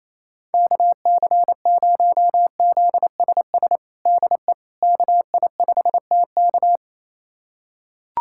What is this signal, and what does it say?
Morse code, heard as a keyed tone, about 28 wpm, 700 Hz. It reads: KC0ZSS DE KI5TK
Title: USA Callsign DE Callsign 28wpm